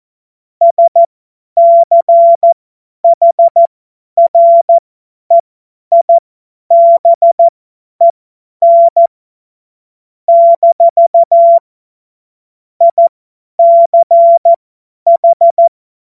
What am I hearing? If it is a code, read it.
Morse code, heard as a keyed tone, about 14 wpm, spaced as for 10 wpm, 680 Hz. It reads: SCHREIBEN - ICH
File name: Morsekurs / Witze / Witz 13. September 2022 50 F70